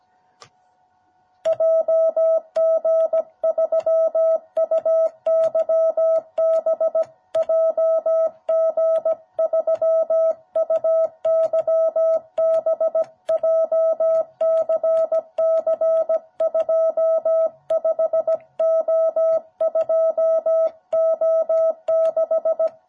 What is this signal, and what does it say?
Morse code, heard as a keyed tone, about 17 wpm, 660 Hz. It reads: JG3UYB JG3UYB JCC25O2O6
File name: jh8cbh / 2021cw